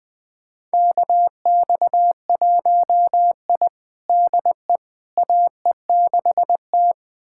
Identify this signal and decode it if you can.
Morse code, heard as a keyed tone, about 20 wpm, 700 Hz. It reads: KX1I DE AE6T
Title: USA Callsign DE Callsign - 4 Characters - 20wpm